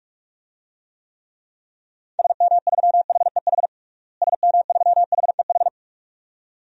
Morse code, heard as a keyed tone, about 41 wpm, 700 Hz. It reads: SM3HEH SM3HEH
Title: DXCC-100 Call Signs - 6 Characters - Repeated 2x - 45wpm